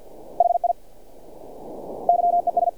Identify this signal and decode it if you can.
Morse code, heard as a keyed tone, about 48 wpm, 700 Hz. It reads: HI YES
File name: archived / CRSnet